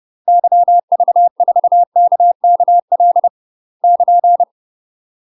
Morse code, heard as a keyed tone, about 30 wpm, 700 Hz. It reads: YV4KKL <KN>
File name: UK MASTER QSOs TRAINING / 30WPM / UK MASTER QSOs 0051